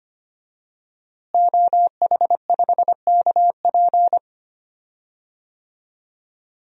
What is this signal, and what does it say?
Morse code, heard as a keyed tone, about 25 wpm, 700 Hz. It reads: OH5KP
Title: DXCC-100 Call Signs - 5 Characters - Repeated 2x - 25wpm